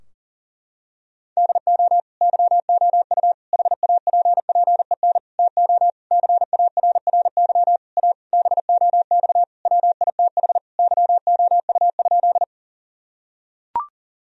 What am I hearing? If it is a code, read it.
Morse code, heard as a keyed tone, about 40 wpm, 700 Hz. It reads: DO YOU HAPPEN TO CARRY A BOX WITH YOU?
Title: Sentences from Top 1000 Words - Review Part 2 - 40wpm